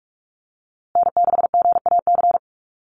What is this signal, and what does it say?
Morse code, heard as a keyed tone, about 44 wpm, 690 Hz. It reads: N6GAC
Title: USA Call Signs Repeated 3x 45wpm